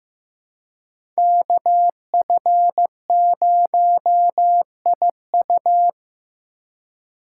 Morse code, heard as a keyed tone, about 15 wpm, 700 Hz. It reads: KF0IU
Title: USA Call Signs - 5 Characters - 3x Character Spacing - 15wpm